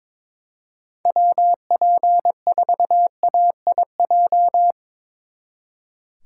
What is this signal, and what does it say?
Morse code, heard as a keyed tone, about 25 wpm, 700 Hz. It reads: WP4AIJ
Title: Speed Racing - USA 2x3 Call Signs 22wpm